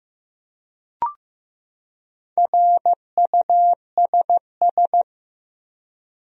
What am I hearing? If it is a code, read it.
Morse code, heard as a keyed tone, about 15 wpm, 700 Hz. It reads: RUSS
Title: USA Common Names - 15wpm